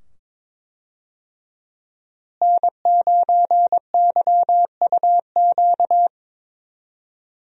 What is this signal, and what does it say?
Morse code, heard as a keyed tone, about 22 wpm, 700 Hz. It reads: N9YUQ
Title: USA Call Signs - 5 Characters - Repeated 2x - 22wpm